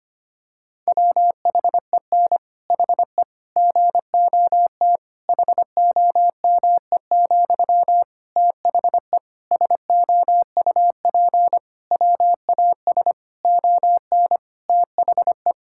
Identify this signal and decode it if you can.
Morse code, heard as a keyed tone, about 25 wpm, 700 Hz. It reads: WHEN HE GOT HOME, THE SOUP WAS ON THE
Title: Sentences from Top 1000 Words - Review Part 6 - 25wpm